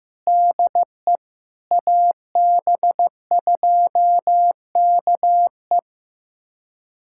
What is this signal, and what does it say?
Morse code, heard as a keyed tone, about 16 wpm, 700 Hz. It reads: DE AB2KE
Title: USA Callsign DE Callsign - 5 Characters - 15wpm